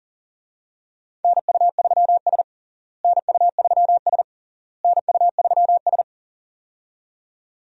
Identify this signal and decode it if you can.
Morse code, heard as a keyed tone, about 40 wpm, 700 Hz. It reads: NU3S NU3S NU3S
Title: USA Call Signs - 4 Characters - Repeated 3x - 40wpm